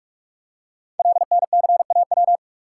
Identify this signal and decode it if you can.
Morse code, heard as a keyed tone, about 44 wpm, 700 Hz. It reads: LNCAW
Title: Code Groups - 5 Letters - 45wpm